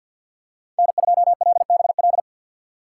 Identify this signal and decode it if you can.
Morse code, heard as a keyed tone, about 52 wpm, 700 Hz. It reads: N2LBL
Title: USA Call Signs - 5 Characters - 6x Character Spacing - 50wpm